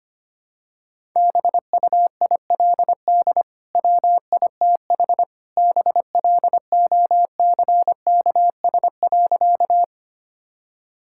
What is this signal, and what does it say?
Morse code, heard as a keyed tone, about 25 wpm, 700 Hz. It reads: BUILD WITH BLOCKS.
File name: Easy Sentences with Hints - 25wpm